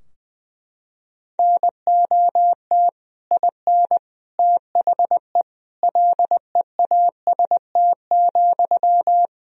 Morse code, heard as a keyed tone, about 20 wpm, 700 Hz. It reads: NOT IN THE LEAST,
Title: Sentences from Top 1000 Words - Review Part 4 - 20wpm